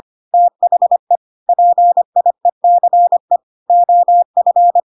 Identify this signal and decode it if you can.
Morse code, heard as a keyed tone, about 25 wpm, 700 Hz. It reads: THE PIECE OF